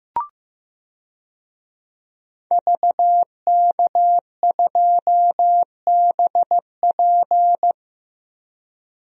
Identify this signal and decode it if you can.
Morse code, heard as a keyed tone, about 15 wpm, 700 Hz. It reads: VK2BP
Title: DXCC-100 Call Signs - 5 Characters - Repeated 2x - 15wpm